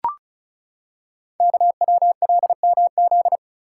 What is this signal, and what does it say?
Morse code, heard as a keyed tone, about 36 wpm, 700 Hz. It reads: KWLMZ
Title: Code Groups - 5 Letter:Numbers - 35wpm